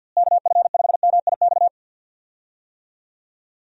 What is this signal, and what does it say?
Morse code, heard as a keyed tone, about 47 wpm, 700 Hz. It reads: KF5MIX
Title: USA Call Signs - 6 Characters - Repeated 3x - 50wpm